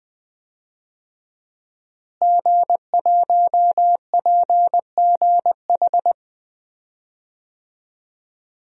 Morse code, heard as a keyed tone, about 20 wpm, 700 Hz. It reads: G1PGH
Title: DXCC-100 Call Signs - 5 Characters - Repeated 2x - 20wpm